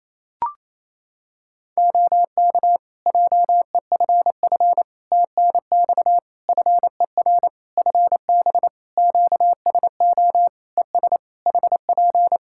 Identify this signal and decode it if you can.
Morse code, heard as a keyed tone, about 28 wpm, 700 Hz. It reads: OK JEFF TNX FER FB QSO ES HP
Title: Practice QSO - Exchange 4 - Ending - 28wpm